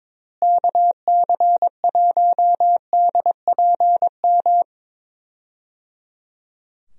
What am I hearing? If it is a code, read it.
Morse code, heard as a keyed tone, about 21 wpm, 700 Hz. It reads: KC1DPM
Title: USA Call Signs - 6 Characters - Repeated 3x - 22wpm